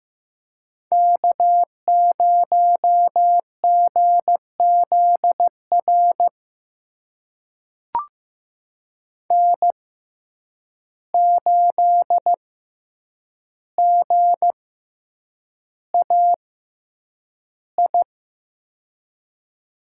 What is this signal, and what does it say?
Morse code, heard as a keyed tone, about 15 wpm, spaced as for 8 wpm, 700 Hz. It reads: K0GZR N8GAI
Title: USA Call Signs - 5 Characters - 6x Character Spacing - 15wpm